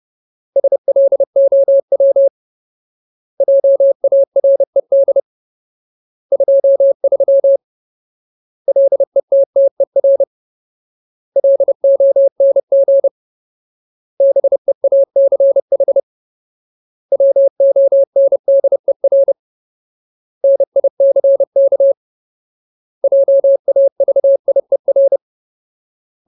Morse code, heard as a keyed tone, about 30 wpm, 550 Hz. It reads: SLOW JARED 23 LETTER LONG BEACH WONDER NICK JAVIER
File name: Ultimate Word List / 30WPM Ultimate